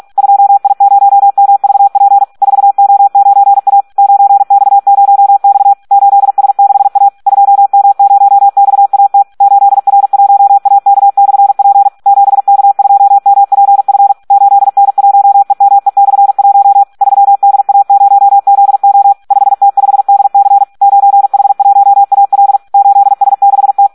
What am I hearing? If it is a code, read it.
Morse code, heard as a keyed tone, about 46 wpm, 800 Hz. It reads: 2I0MVP 4O9A 9X0X 8S6A 2M0XAT 8R1AK/P 7K2MPF 8N1EME/1 3DA0ZO 5T5DC 9H1AL 8S6T
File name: Audio practica CW / SD 1 46